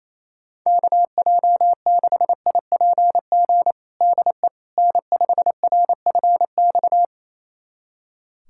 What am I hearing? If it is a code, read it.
Morse code, heard as a keyed tone, about 28 wpm, 700 Hz. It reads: KJ6IPG DE N5RFX
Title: USA Callsign DE Callsign 28wpm